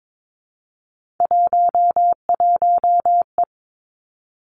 Morse code, heard as a keyed tone, about 22 wpm, 700 Hz. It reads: 11E
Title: Three Challenging Letter-Number Combinations 22wpm